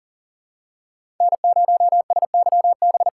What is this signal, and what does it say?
Morse code, heard as a keyed tone, about 40 wpm, 700 Hz. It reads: N0SYB
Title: USA Callsign DE Callsign - 5 Characters - 40wpm